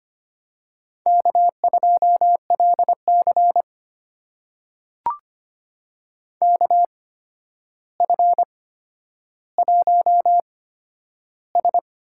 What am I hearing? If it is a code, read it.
Morse code, heard as a keyed tone, about 25 wpm, spaced as for 11 wpm, 700 Hz. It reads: K2LC KF1S
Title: USA Call Signs - 4 Characters - 8x Character Spacing - 25wpm